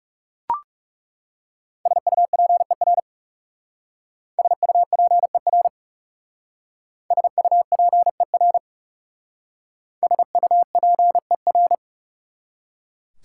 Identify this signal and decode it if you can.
Morse code, heard as a keyed tone, about 37 wpm, 690 Hz. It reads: SUPER SUPER SUPER SUPER
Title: Speed-Racing Prefixes and Suffixes 30wpm